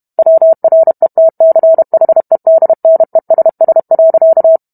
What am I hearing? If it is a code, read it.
Morse code, heard as a keyed tone, about 32 wpm, 650 Hz. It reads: WRETCHEDNESS.